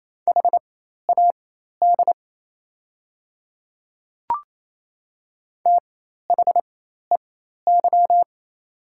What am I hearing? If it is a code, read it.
Morse code, heard as a keyed tone, about 28 wpm, spaced as for 13 wpm, 700 Hz. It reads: HAD THEY
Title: 4x Character Spacing - Top 100 Words - 28wpm